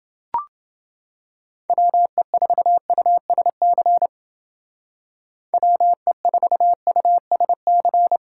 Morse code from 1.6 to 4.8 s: WE4USC

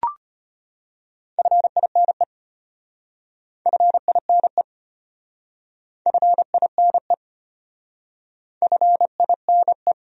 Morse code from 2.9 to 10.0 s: FINE FINE FINE

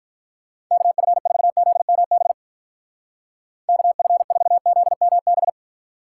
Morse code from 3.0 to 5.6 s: KF4ZMB

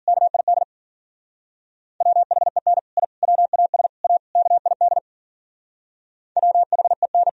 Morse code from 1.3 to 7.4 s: WHEN I WAS A KID WHEN

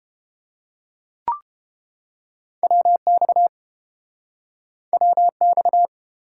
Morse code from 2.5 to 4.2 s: WX

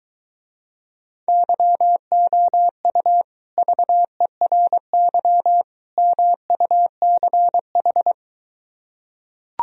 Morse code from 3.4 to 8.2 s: VERY MUCH

